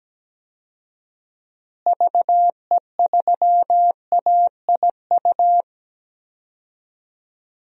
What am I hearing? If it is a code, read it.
Morse code, heard as a keyed tone, about 17 wpm, 700 Hz. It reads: VE3AIU
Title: DXCC-100 Call Signs - 6 Characters - Repeated 2x - 17wpm